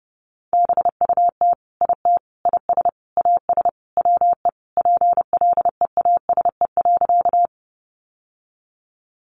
Morse code, heard as a keyed tone, about 30 wpm, 700 Hz. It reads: BUT IT IS AS WE PLEASE.
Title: Sentences from Top 1500 Words 30wpm